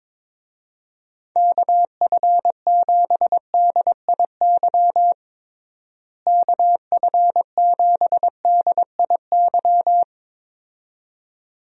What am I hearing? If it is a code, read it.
Morse code, heard as a keyed tone, about 22 wpm, 700 Hz. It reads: KF7DIY KF7DIY
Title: USA Call Signs - 6 Characters - Repeated 2x - 22wpm